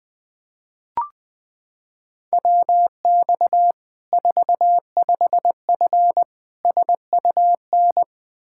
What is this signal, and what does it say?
Morse code, heard as a keyed tone, about 20 wpm, 700 Hz. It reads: WX 45F SUN
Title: Practice QSO - Exchange 2 - Info - 20wpm